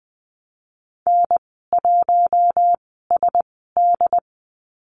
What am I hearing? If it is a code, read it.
Morse code, heard as a keyed tone, about 20 wpm, spaced as for 14 wpm, 700 Hz. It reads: N1SD
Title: 069 Lesson 12g - Letter d - Call Signs 20wpm